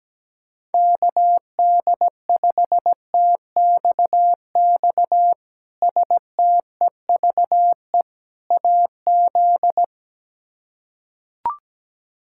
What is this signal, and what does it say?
Morse code, heard as a keyed tone, about 17 wpm, 700 Hz. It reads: KD5TXX STEVE AZ
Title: K1USN SST QSO Practice - 17wpm